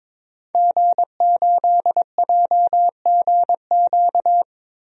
Morse code, heard as a keyed tone, about 21 wpm, 700 Hz. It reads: G8JGQ
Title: DXCC-100 Call Signs - 5 Characters - Repeated 2x - 22wpm